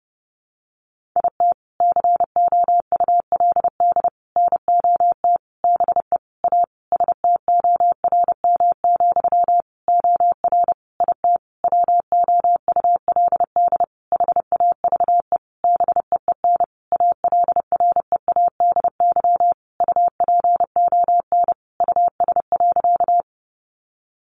Morse code, heard as a keyed tone, about 30 wpm, 700 Hz. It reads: IT COULD NOT BE A STORM, OR IT WOULD HAVE BEEN ALREADY UPON US.
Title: Sentences from Top 1500 Words 30wpm